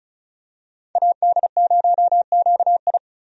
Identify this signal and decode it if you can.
Morse code, heard as a keyed tone, about 34 wpm, 700 Hz. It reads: AD0QI
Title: USA Call Signs - 5 Characters - 35wpm